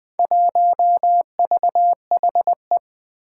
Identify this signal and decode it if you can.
Morse code, heard as a keyed tone, about 20 wpm, 700 Hz. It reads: 1VHE
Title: Four Challenging Letter-Number Combinations 20wpm